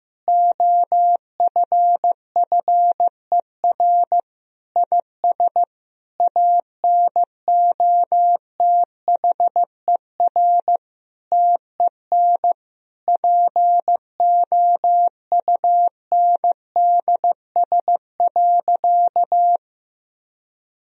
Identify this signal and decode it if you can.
Morse code, heard as a keyed tone, about 15 wpm, 700 Hz. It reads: OFFER IS ANOTHER TEN POUNDS.
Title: Sentences from Top 1000 Words - Review Part 4 - 15wpm